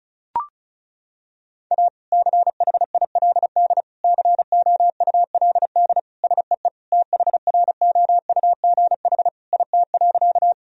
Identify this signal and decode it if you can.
Morse code, heard as a keyed tone, about 36 wpm, 700 Hz. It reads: A CHILD COULD SEE THROUGH IT.
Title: Sentences from Top 1000 Words - Review Part 1 - 35wpm